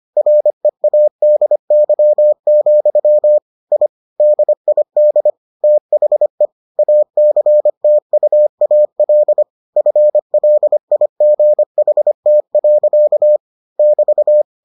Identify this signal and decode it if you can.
Morse code, heard as a keyed tone, about 25 wpm, 600 Hz. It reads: READY, I DID THE ACTUAL FLIGHT. <BT>